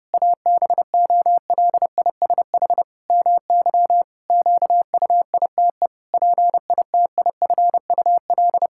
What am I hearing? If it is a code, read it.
Morse code, heard as a keyed tone, about 30 wpm, 700 Hz. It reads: ABOLISH MY QUITE PITIFUL
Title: Pangram Challenge v2 - 30wpm